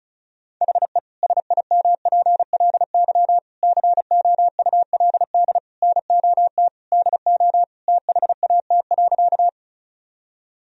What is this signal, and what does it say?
Morse code, heard as a keyed tone, about 36 wpm, 700 Hz. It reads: HE SIMPLY COULD NOT DO THAT.
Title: Sentences from Top 1000 Words - Review Part 4 - 35wpm